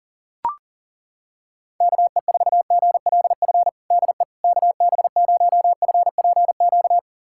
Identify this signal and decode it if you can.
Morse code, heard as a keyed tone, about 37 wpm, 700 Hz. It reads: KE4GLF DE KB0FPQ
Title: USA Callsign DE Callsign 40wpm